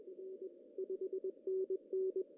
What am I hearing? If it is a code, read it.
Morse code, heard as a keyed tone, about 21 wpm, 380 Hz, with radio noise. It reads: R 5NN
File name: CW / cqww-cw-k5tr